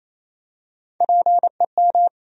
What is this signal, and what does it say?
Morse code, heard as a keyed tone, about 27 wpm, 700 Hz. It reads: PEM
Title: USA Call Sign Postfixes - 3 Letters 28wpm